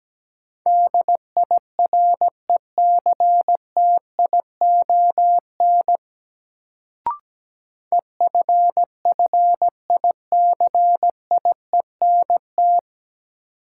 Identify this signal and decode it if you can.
Morse code, heard as a keyed tone, about 17 wpm, 700 Hz. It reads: DIRECTION EFFICIENT
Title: Sentences from Top 1300 Words - New Word Drill Alt - 17wpm